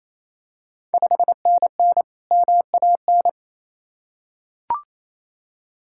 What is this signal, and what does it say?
Morse code, heard as a keyed tone, about 28 wpm, 700 Hz. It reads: 5NN MAN
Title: Ontario QSO Party Exchange 28wpm